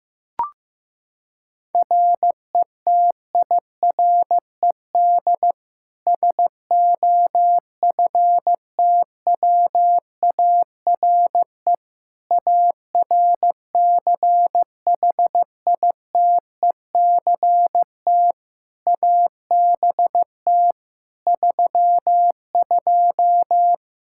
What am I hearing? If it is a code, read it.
Morse code, heard as a keyed tone, about 15 wpm, 700 Hz. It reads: RETIRED SOFTWARE ARCHITECT ABT 32
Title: Practice QSO - Exchange 3 - More Info - 15wpm